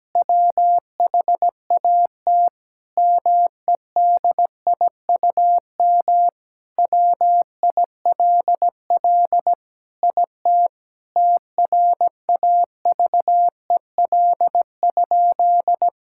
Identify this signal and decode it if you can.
Morse code, heard as a keyed tone, about 17 wpm, 700 Hz. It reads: WHAT MEDIUM WILL IT TRAVEL?
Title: Sentences from Top 1000 Words - Review Part 6 - 17wpm